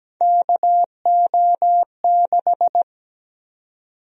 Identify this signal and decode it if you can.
Morse code, heard as a keyed tone, about 17 wpm, 700 Hz. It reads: KO6
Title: USA Call Sign Prefixes - 17wpm